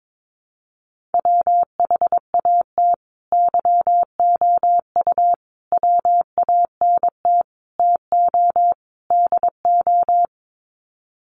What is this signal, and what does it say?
Morse code, heard as a keyed tone, about 22 wpm, 700 Hz. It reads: WHAT YOU WANT TO DO
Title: Sets of 5 Words Encore 22wpm